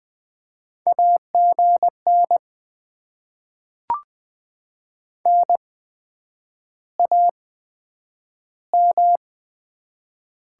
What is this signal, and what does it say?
Morse code, heard as a keyed tone, about 20 wpm, 700 Hz. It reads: A G N N A M
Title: Top 20 QSO Elements - 8x Character Spacing - 20wpm